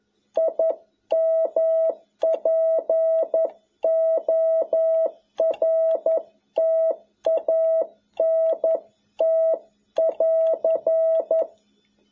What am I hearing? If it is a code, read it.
Morse code, heard as a keyed tone, about 11 wpm, 650 Hz. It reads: IMPORTANT <AR>
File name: jh8cbh / 2021cw